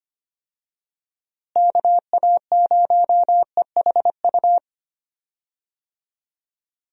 Morse code, heard as a keyed tone, about 25 wpm, 700 Hz. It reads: KA0EHU
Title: USA Call Signs - 6 Characters - Repeated 3x - 25wpm